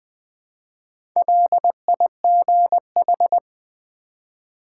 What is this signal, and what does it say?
Morse code, heard as a keyed tone, about 20 wpm, 700 Hz. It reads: LIGH
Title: Build A Word - 1 Letter At A Time - 5-Letter Words - 20wpm